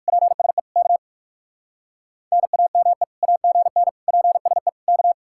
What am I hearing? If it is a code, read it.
Morse code, heard as a keyed tone, about 60 wpm, 700 Hz. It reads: PSE K NAME AGN PSE K